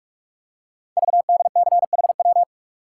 Morse code, heard as a keyed tone, about 44 wpm, 700 Hz. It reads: VDCHW
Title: Code Groups - 5 Letters - 45wpm